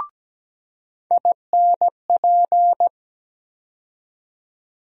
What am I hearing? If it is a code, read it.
Morse code, heard as a keyed tone, about 17 wpm, 700 Hz. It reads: INP
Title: USA Call Sign Postfixes - 3 Letters 17wpm